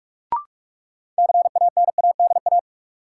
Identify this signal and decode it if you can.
Morse code, heard as a keyed tone, about 44 wpm, 700 Hz. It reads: CANADA
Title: Sets of 5 Words - New Word Drill - 45wpm